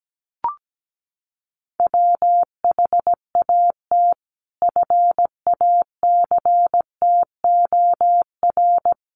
Morse code, heard as a keyed tone, about 17 wpm, 700 Hz. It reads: WHAT FACTOR
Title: Sentences from Top 1300 Words 17wpm